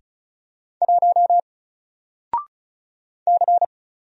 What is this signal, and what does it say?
Morse code, heard as a keyed tone, about 36 wpm, 700 Hz. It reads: 1 C
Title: Single Letter-Number 35wpm